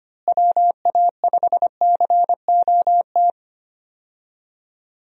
Morse code, heard as a keyed tone, about 25 wpm, 700 Hz. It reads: WA5COT
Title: USA Call Signs - 6 Characters - Repeated 3x - 25wpm